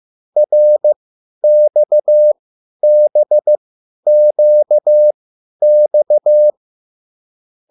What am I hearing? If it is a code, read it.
Morse code, heard as a keyed tone, about 15 wpm, spaced as for 10 wpm, 600 Hz. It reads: RXBQX